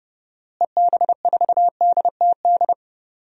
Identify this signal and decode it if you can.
Morse code, heard as a keyed tone, about 30 wpm, 700 Hz. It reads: EB4DTD
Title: DXCC-100 Call Signs - 6 Characters - Repeated 3x - 30wpm